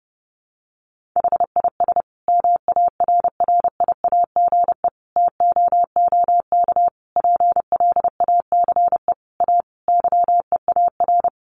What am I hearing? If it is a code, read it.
Morse code, heard as a keyed tone, about 30 wpm, 700 Hz. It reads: HIS MARRIAGE TOOK PLACE A YEAR
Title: Sentences from Top 1500 Words 30wpm